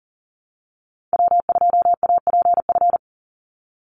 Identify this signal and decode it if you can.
Morse code, heard as a keyed tone, about 40 wpm, 700 Hz. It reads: W2APF
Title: Call Signs Repeated 3x - 40wpm